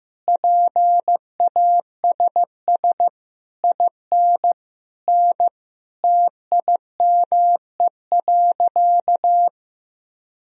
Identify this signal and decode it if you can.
Morse code, heard as a keyed tone, about 15 wpm, 700 Hz. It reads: PASS IN N TIME.
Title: Sentences from Top 400 Words and Abbreviations 15wpm